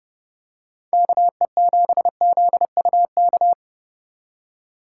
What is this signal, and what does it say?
Morse code, heard as a keyed tone, about 30 wpm, 700 Hz. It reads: KE7ZUK
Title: USA Call Signs - 6 Characters - 4x Character Spacing - 30wpm